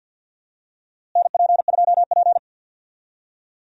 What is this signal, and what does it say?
Morse code, heard as a keyed tone, about 52 wpm, 700 Hz. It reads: NP2P
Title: USA Call Signs - 4 Characters - Repeated 3x - 50wpm